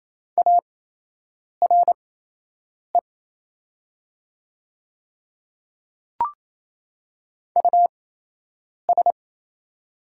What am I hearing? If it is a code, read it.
Morse code, heard as a keyed tone, about 27 wpm, spaced as for 8 wpm, 690 Hz. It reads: ARE US
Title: 8x Character Spacing - Top 300 Words - 28wpm